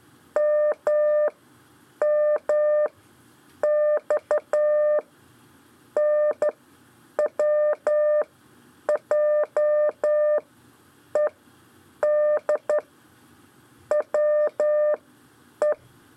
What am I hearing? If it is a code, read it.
Morse code, heard as a keyed tone, about 12 wpm, 600 Hz. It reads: M M X N W J E D W E
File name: 4 LEZIONE CW 1